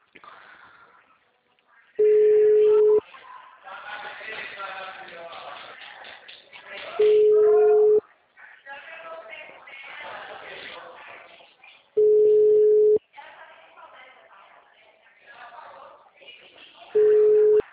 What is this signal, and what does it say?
Ringback tone (Europe, 425 Hz), with speech.